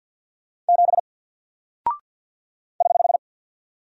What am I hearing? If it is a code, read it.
Morse code, heard as a keyed tone, about 48 wpm, 700 Hz. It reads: 7 <HH>
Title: Single Character (Letter:Number:Punctuation:Prosign) Alt 50wpm